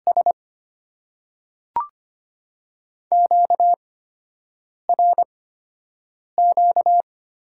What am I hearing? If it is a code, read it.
Morse code, heard as a keyed tone, about 25 wpm, spaced as for 7 wpm, 690 Hz. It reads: S QRQ
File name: Top 90 QSO Elements - 8x Character Spacing - 25wpm